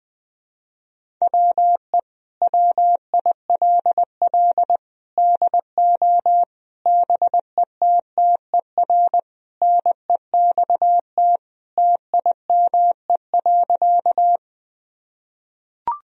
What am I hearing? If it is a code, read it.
Morse code, heard as a keyed tone, about 20 wpm, 700 Hz. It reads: WE WILL DO BETTER NEXT TIME.
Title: Sentences from Top 1000 Words - Review Part 1 - 20wpm